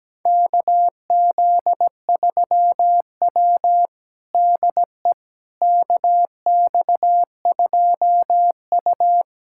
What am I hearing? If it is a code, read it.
Morse code, heard as a keyed tone, about 17 wpm, 700 Hz. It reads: KZ3W DE KX2U
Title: USA Callsign DE Callsign - 4 Characters - 17wpm